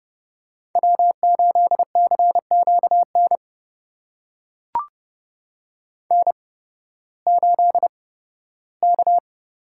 Morse code, heard as a keyed tone, about 32 wpm, spaced as for 16 wpm, 700 Hz. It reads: W8CQN N8K
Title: USA Call Signs - 5 Characters - 8x Character Spacing - 30wpm